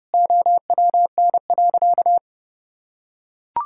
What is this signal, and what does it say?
Morse code, heard as a keyed tone, about 32 wpm, 700 Hz. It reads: OWN.
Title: Sentences from Top 1000 Words - Review Part 6 - 30wpm